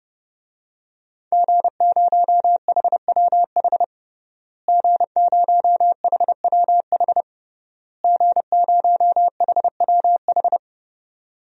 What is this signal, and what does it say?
Morse code, heard as a keyed tone, about 32 wpm, 700 Hz. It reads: G0HWH G0HWH G0HWH
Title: DXCC-100 Call Signs - 5 Characters - Repeated 3x - 30wpm